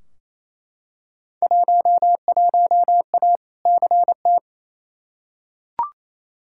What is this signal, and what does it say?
Morse code, heard as a keyed tone, about 28 wpm, 700 Hz. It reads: 11A CT
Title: Field Day Exchanges Encore - 28wpm